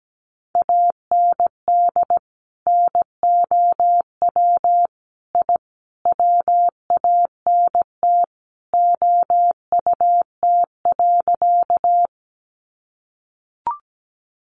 Morse code, heard as a keyed tone, about 17 wpm, 700 Hz. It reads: AND NOW I WANT OUT.
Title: A Sound Of Thunder 17wpm